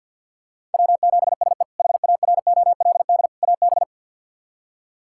Morse code, heard as a keyed tone, about 47 wpm, 690 Hz. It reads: W7SE HAROLD AB
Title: K1USN SST QSO Practice - 50wpm